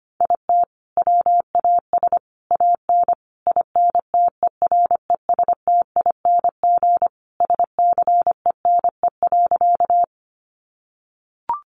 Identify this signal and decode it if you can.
Morse code, heard as a keyed tone, about 25 wpm, 700 Hz. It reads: IT WAS AN INTERESTING SCENE.
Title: Sentences from Top 1100 Words 25wpm